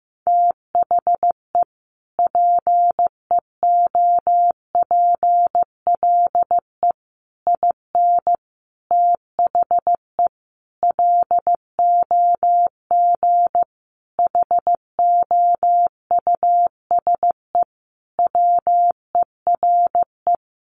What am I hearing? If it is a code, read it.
Morse code, heard as a keyed tone, about 14 wpm, 700 Hz. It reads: THE PEOPLE IN THE LOG HOUSE WERE
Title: Sentences from Top 1400 Words 15wpm